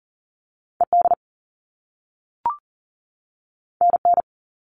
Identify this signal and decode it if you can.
Morse code, heard as a keyed tone, about 38 wpm, 700 Hz. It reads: ED NN
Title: All Cut Numbers - 2-Digit 40wpm